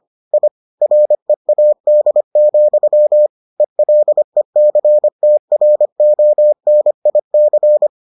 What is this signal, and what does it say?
Morse code, heard as a keyed tone, about 25 wpm, 600 Hz. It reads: I READ, ELECTRONIC